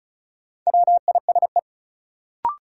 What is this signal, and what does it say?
Morse code, heard as a keyed tone, about 36 wpm, 700 Hz. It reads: WISE
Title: Sentences from Top 1600 Words - New Word Drill Alt - 35wpm